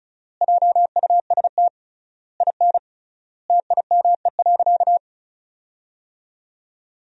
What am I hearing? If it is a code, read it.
Morse code, heard as a keyed tone, about 36 wpm, 700 Hz. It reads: JUST IN TIME.
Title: 3x Word-Spacing - Sentences from Top 300 Words - 35wpm